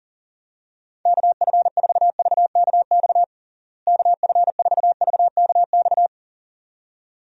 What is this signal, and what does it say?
Morse code, heard as a keyed tone, about 37 wpm, 700 Hz. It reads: KF4VKX KF4VKX
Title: USA Call Signs - 6 Characters - Repeated 2x - 40wpm